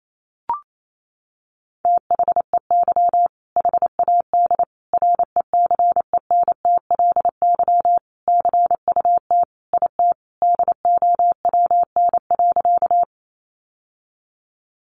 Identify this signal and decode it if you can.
Morse code, heard as a keyed tone, about 28 wpm, 700 Hz. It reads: THEY HAD RECENTLY CUT IT DOWN.
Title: Sentences from Top 1200 Words 28wpm